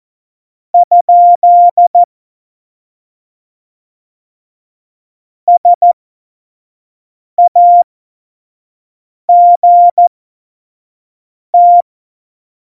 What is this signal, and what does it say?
Morse code, heard as a keyed tone, about 14 wpm, spaced as for 5 wpm, 700 Hz. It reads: ? SAGT
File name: Morsekurs / Witze / Witz 12. Dezember 2017, 25 F70